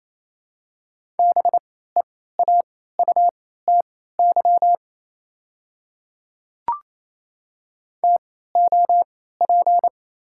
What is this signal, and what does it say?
Morse code, heard as a keyed tone, about 28 wpm, spaced as for 16 wpm, 690 Hz. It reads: BEAUTY TOP
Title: 3x Character Spacing - Top 500 Words - 28wpm